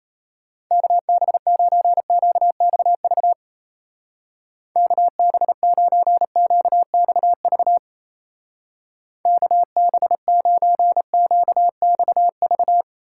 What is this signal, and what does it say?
Morse code, heard as a keyed tone, about 36 wpm, 700 Hz. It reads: KB9QXV KB9QXV KB9QXV